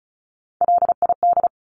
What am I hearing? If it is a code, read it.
Morse code, heard as a keyed tone, about 34 wpm, 700 Hz. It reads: LID
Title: Abbreviations Repeated 3x - 35wpm